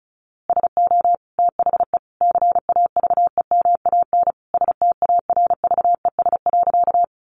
Morse code, heard as a keyed tone, about 34 wpm, 700 Hz. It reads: SO THE CAVEMAN STARVES.
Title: A Sound Of Thunder 35wpm